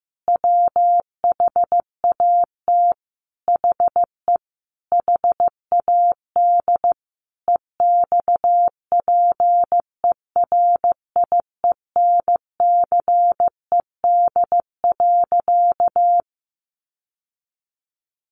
Morse code, heard as a keyed tone, about 15 wpm, 700 Hz. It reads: WHAT HE HAD EXPERIENCED.
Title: Sentences from Top 1100 Words 15wpm